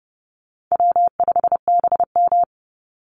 Morse code, heard as a keyed tone, about 30 wpm, 700 Hz. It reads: W5BM
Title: Easy Call Signs - 30wpm